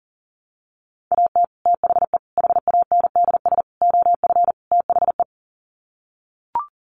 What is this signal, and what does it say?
Morse code, heard as a keyed tone, about 40 wpm, 700 Hz. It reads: AT THE HANDS OF THE
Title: Sets of 5 Words Encore2 40wpm